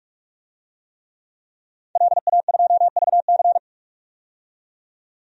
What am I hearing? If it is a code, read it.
Morse code, heard as a keyed tone, about 45 wpm, 690 Hz. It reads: LA2VC
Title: DXCC-100 Call Signs - 5 Characters - 45wpm